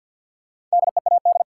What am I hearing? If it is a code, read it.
Morse code, heard as a keyed tone, about 53 wpm, 700 Hz. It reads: DEAD
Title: Sentences from Top 1500 Words - New Word Drill Alt - 50wpm